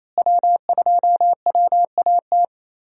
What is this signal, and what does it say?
Morse code, heard as a keyed tone, about 27 wpm, 700 Hz. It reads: W2WAT
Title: USA Call Signs - 5 Characters - Repeated 3x - 28wpm